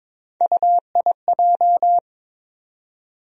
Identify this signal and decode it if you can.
Morse code, heard as a keyed tone, about 22 wpm, 700 Hz. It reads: UIJ
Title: USA Call Sign Postfixes - 3 Letters 22wpm